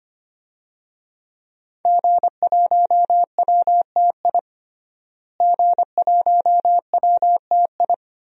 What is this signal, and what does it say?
Morse code, heard as a keyed tone, about 25 wpm, 700 Hz. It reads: G1WTI G1WTI
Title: DXCC-100 Call Signs - 5 Characters - Repeated 2x - 25wpm